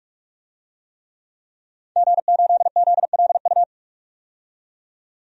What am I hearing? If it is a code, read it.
Morse code, heard as a keyed tone, about 46 wpm, 700 Hz. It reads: G8ZLU
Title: DXCC-100 Call Signs - 5 Characters - 45wpm